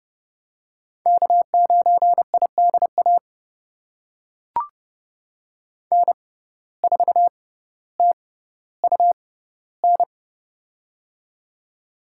Morse code, heard as a keyed tone, about 30 wpm, spaced as for 15 wpm, 700 Hz. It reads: K9IDA N4TUN
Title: USA Call Signs - 6x Character Spacing - 30wpm